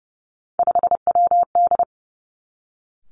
Morse code, heard as a keyed tone, about 30 wpm, 700 Hz. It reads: 5WD
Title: Three Challenging Letter-Number Combinations 30wpm